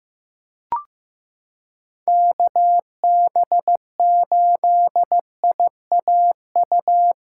Morse code, heard as a keyed tone, about 15 wpm, 700 Hz. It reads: KB8IAU
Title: USA Callsign DE Callsign - 6 Characters - 15wpm